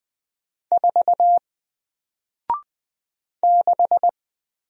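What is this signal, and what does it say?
Morse code, heard as a keyed tone, about 20 wpm, 700 Hz. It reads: 4 6
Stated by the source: Single letter and number Morse drill at 20 wpm, from N4LCD.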